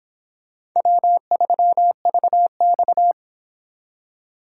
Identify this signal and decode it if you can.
Morse code, heard as a keyed tone, about 26 wpm, 700 Hz. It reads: W3VX